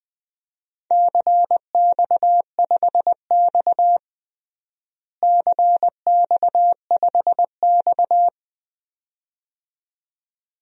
Morse code, heard as a keyed tone, about 20 wpm, 700 Hz. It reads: CX5X CX5X
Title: DXCC-100 Call Signs - 4 Characters - Repeated 2x - 20wpm